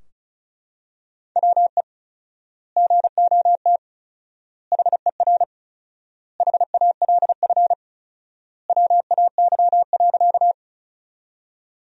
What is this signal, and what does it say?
Morse code, heard as a keyed tone, about 34 wpm, 700 Hz. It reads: WE GOT HER HALF WAY.
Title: 4x Word-Spacing - Sentences from Top 400 Words - 35wpm